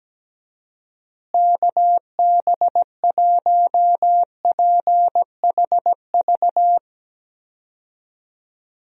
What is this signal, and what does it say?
Morse code, heard as a keyed tone, about 17 wpm, 700 Hz. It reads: KB1PHV
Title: USA Call Signs - 6 Characters - Repeated 2x - 17wpm